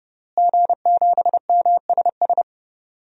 Morse code, heard as a keyed tone, about 32 wpm, 700 Hz. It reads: G7MSS
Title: DXCC-100 Call Signs - 5 Characters - Repeated 2x - 30wpm